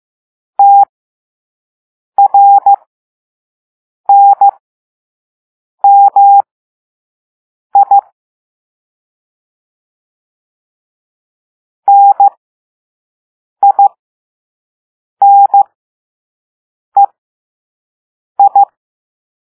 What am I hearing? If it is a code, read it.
Morse code, heard as a keyed tone, about 15 wpm, spaced as for 5 wpm, 800 Hz. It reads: TRNMI NINEI
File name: Morse / Level2-All-A